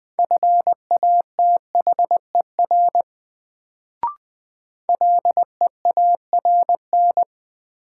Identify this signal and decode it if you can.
Morse code, heard as a keyed tone, about 20 wpm, 700 Hz. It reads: FATHER LEARN
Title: Sentences from Top 300 Words - New Word Drill - 20wpm